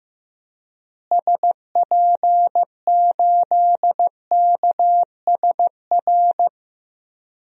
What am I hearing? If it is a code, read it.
Morse code, heard as a keyed tone, about 15 wpm, 700 Hz. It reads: SP8KSR
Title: DXCC-100 Call Signs - 6 Characters - Repeated 3x - 15wpm